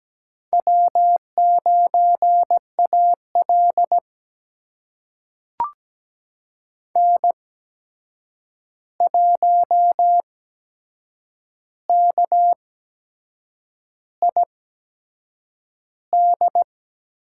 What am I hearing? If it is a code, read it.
Morse code, heard as a keyed tone, about 17 wpm, spaced as for 7 wpm, 700 Hz. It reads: W9AL N1KID
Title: USA Call Signs - 8x Character Spacing - 17wpm